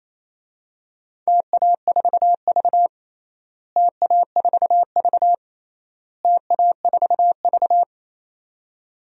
Morse code, heard as a keyed tone, about 28 wpm, 700 Hz. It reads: TA4V TA4V TA4V
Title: DXCC-100 Call Signs - 4 Characters - Repeated 3x - 28wpm